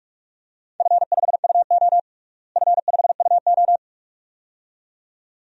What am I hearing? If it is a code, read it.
Morse code, heard as a keyed tone, about 41 wpm, 700 Hz. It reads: F5UO F5UO
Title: DXCC-100 Call Signs - 4 Characters - Repeated 2x - 45wpm